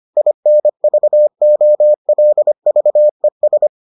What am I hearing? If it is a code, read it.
Morse code, heard as a keyed tone, about 25 wpm, 600 Hz. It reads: INVOLVES